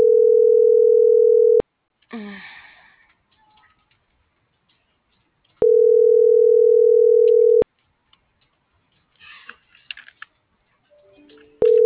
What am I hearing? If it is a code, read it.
Ringback tone (North America), 440 + 480 Hz.